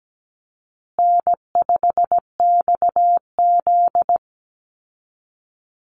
Morse code, heard as a keyed tone, about 17 wpm, 700 Hz. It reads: N5XZ
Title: Easy Call Signs Repeated 3x - 17wpm